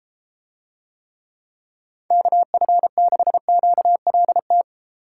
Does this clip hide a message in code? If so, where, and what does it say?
Yes: Morse code from 2.1 to 4.6 s, heard as a keyed tone, about 32 wpm, 700 Hz. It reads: KF6QLT